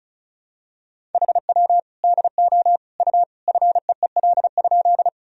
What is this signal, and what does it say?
Morse code, heard as a keyed tone, about 36 wpm, 700 Hz. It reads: HW DO U FEEL?
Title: Sentences from Top 400 Words and Abbreviations 35wpm